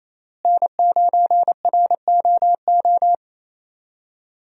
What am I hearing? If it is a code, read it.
Morse code, heard as a keyed tone, about 27 wpm, 700 Hz. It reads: N9ROO
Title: USA Call Signs - 5 Characters - Repeated 3x - 28wpm